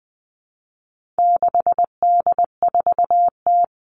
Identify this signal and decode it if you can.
Morse code, heard as a keyed tone, about 20 wpm, 700 Hz. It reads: 6D4T
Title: All Cut Numbers - 4-Digit 20wpm